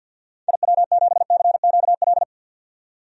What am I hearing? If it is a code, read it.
Morse code, heard as a keyed tone, about 52 wpm, 700 Hz. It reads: IW7CQL
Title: DXCC-100 Call Signs - 6 Characters - 50wpm